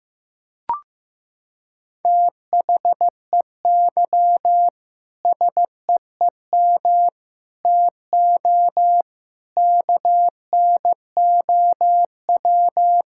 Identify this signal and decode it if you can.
Morse code, heard as a keyed tone, about 15 wpm, 700 Hz. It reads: THEY SEEM TO KNOW
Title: Sentences from Top 1000 Words - Review Part 6 - 15wpm